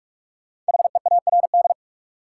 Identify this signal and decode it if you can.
Morse code, heard as a keyed tone, about 46 wpm, 700 Hz. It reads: HEARD
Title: Sentences from Top 400 Words - New Word Drill - 45wpm